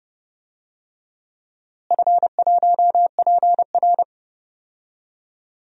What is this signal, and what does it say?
Morse code, heard as a keyed tone, about 30 wpm, 700 Hz. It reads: F1PR
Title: DXCC-100 Call Signs - 4 Characters - Repeated 3x - 30wpm